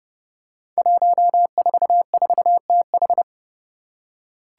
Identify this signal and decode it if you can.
Morse code, heard as a keyed tone, about 30 wpm, 700 Hz. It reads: 144TH
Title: Five Challenging Letter-Number Combinations 30wpm